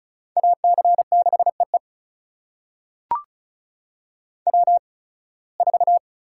Morse code, heard as a keyed tone, about 34 wpm, spaced as for 23 wpm, 700 Hz. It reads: AC6EE W4
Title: USA Call Signs - 5 Characters - 8x Character Spacing - 35wpm